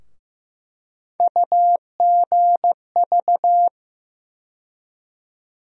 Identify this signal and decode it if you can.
Morse code, heard as a keyed tone, about 15 wpm, 700 Hz. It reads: UGV
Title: USA Call Sign Postfixes - 3 Letters 15wpm